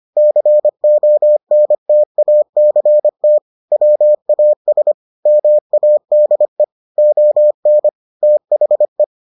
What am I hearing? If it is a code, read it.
Morse code, heard as a keyed tone, about 25 wpm, 600 Hz. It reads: CONTACT WAS MADE ON THE